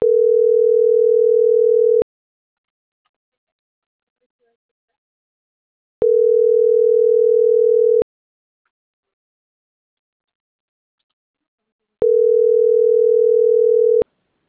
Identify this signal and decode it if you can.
Ringback tone (North America), 440 + 480 Hz.